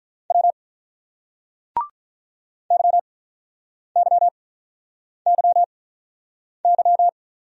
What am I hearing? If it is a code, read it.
Morse code, heard as a keyed tone, about 49 wpm, spaced as for 9 wpm, 700 Hz. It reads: W YYYY